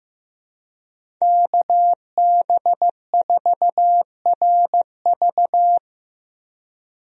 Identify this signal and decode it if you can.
Morse code, heard as a keyed tone, about 15 wpm, 700 Hz. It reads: KB4RV
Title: USA Call Signs - 5 Characters - Repeated 3x - 15wpm